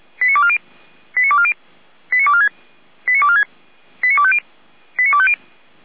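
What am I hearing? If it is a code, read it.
Five-tone selective call (ZVEI), heard as quick runs of five tones, tones of about 70 ms. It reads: 89240 89240 89246 89246 89250 89250
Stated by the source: ZVEI five-tone call received by SDR, from the Feuerwehr Orlamünde audio server.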